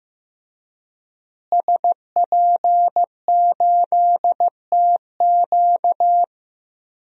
Morse code, heard as a keyed tone, about 16 wpm, 700 Hz. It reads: SP8TQ
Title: DXCC-100 Call Signs - 5 Characters - Repeated 2x - 15wpm